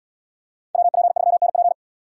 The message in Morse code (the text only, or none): WP3NP